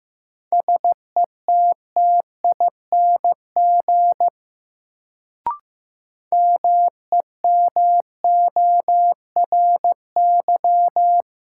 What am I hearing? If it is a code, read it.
Morse code, heard as a keyed tone, about 15 wpm, 700 Hz. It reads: SETTING MEMORY
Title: Sentences from Top 1300 Words - New Word Drill - 15wpm